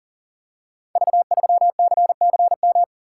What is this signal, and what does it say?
Morse code, heard as a keyed tone, about 42 wpm, 700 Hz. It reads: V3CCM
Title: Code Groups - 5 Letter:Numbers - 40wpm